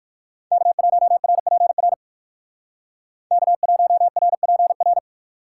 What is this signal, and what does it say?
Morse code, heard as a keyed tone, about 63 wpm, 700 Hz. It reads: K1RPR K1RPR